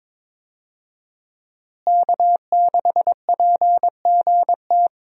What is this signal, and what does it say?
Morse code, heard as a keyed tone, about 21 wpm, 700 Hz. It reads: K6PGT
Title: USA Call Signs - 5 Characters - Repeated 3x - 22wpm